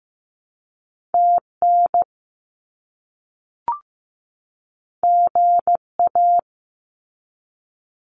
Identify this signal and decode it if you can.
Morse code, heard as a keyed tone, about 15 wpm, 700 Hz. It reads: TN GA
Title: US States 15wpm